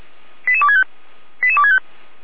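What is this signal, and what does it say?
Five-tone selective call (ZVEI), heard as quick runs of five tones, tones of about 70 ms. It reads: 88265 88265
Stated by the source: ZVEI five-tone call received by SDR.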